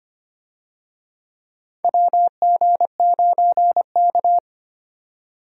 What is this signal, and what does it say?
Morse code, heard as a keyed tone, about 25 wpm, 700 Hz. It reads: WG9K
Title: USA Call Signs - 4 Characters - 25wpm